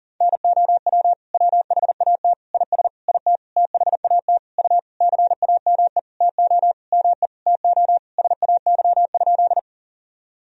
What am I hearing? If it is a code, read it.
Morse code, heard as a keyed tone, about 40 wpm, 700 Hz. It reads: NOW WHAT IS IT THAT U CAME TO ME TO SAY?
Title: Sentences from Top 200 Words and Abbreviations 40wpm